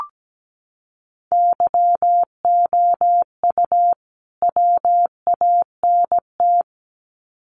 Morse code, heard as a keyed tone, about 17 wpm, 700 Hz. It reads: YOU WANT
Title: Sets of 2 Words 17wpm